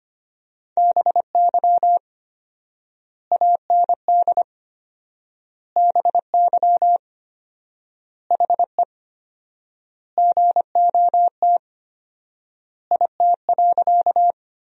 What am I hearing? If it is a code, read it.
Morse code, heard as a keyed tone, about 25 wpm, 700 Hz. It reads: BY AND BY HE GOT IT.
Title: 4x Word-Spacing - Sentences from Top 300 Words - 25wpm